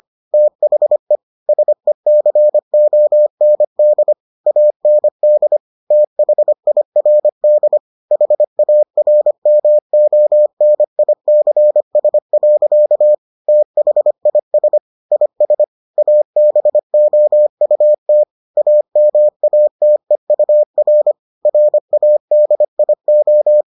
Morse code, heard as a keyed tone, about 25 wpm, 600 Hz. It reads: THE SECOND AND THIRD HARMONICS. THIS IS ABOUT AMATEUR RADIO